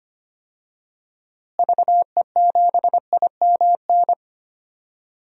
Morse code, heard as a keyed tone, about 25 wpm, 700 Hz. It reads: VE7IMN
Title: DXCC-100 Call Signs - 6 Characters - Repeated 2x - 25wpm